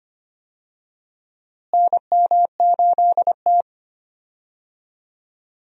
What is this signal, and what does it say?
Morse code, heard as a keyed tone, about 25 wpm, 700 Hz. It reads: NM8T
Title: USA Call Signs - 4 Characters - Repeated 3x - 25wpm